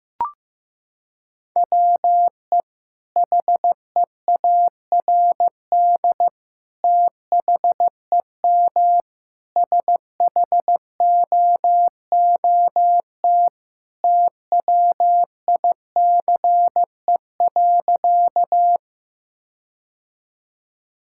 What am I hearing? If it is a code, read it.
Morse code, heard as a keyed tone, about 15 wpm, 700 Hz. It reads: WE HEARD THEM SHOOT TWICE.
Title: Sentences from Top 1600 Words 15wpm